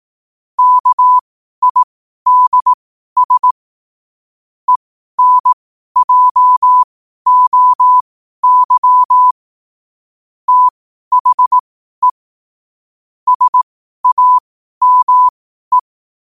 Morse code, heard as a keyed tone, about 18 wpm, spaced as for 12 wpm, 1000 Hz. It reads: KIDS ENJOY THE SAME
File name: qotdfiles / 12wpm